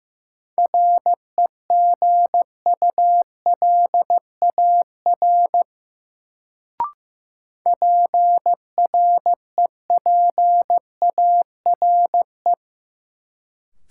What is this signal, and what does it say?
Morse code, heard as a keyed tone, about 15 wpm, 700 Hz. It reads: REGULAR PREPARE
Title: Sentences from Top 900 Words - New Word Drill - 15wpm